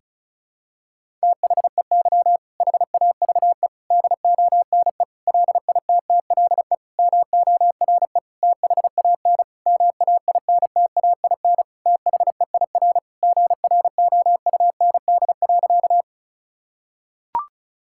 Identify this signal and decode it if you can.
Morse code, heard as a keyed tone, about 36 wpm, 700 Hz. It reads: THEY HAVE DONE LITTLE MORE THAN MAINTAIN THEIR GROUND.
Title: Sentences from Top 1000 Words - Review Part 5 - 35wpm